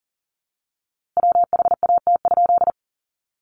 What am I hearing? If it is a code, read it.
Morse code, heard as a keyed tone, about 40 wpm, 700 Hz. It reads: WHAT?
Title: A Sound Of Thunder 40wpm